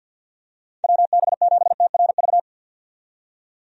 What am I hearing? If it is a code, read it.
Morse code, heard as a keyed tone, about 50 wpm, 700 Hz. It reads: WB7TRV
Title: USA Call Signs - 6 Characters - 50wpm